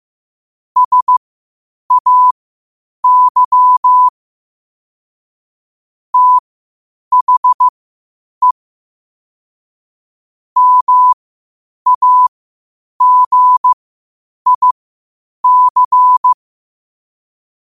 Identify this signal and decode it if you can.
Morse code, heard as a keyed tone, about 15 wpm, spaced as for 8 wpm, 1000 Hz. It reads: SAY THE MAGIC
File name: qotdfiles / 8wpm